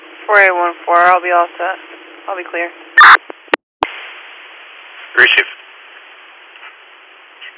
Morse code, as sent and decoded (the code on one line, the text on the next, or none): none